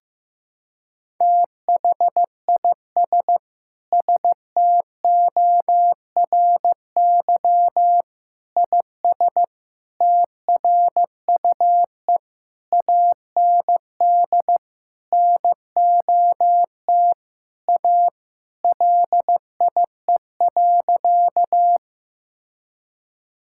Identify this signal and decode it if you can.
Morse code, heard as a keyed tone, about 15 wpm, 700 Hz. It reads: THIS STORY IS TRUE AND NOT A LIE.
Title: Sentences from Top 1600 Words 15wpm